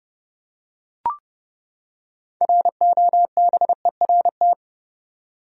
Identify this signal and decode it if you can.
Morse code, heard as a keyed tone, about 30 wpm, 700 Hz. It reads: ROBERT
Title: USA Common Names - 30wpm